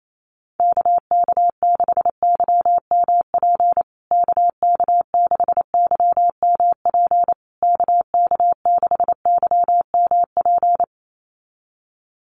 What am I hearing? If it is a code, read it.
Morse code, heard as a keyed tone, about 28 wpm, 700 Hz. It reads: KK6YMP KK6YMP KK6YMP
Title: USA Call Signs Repeated 3x 28wpm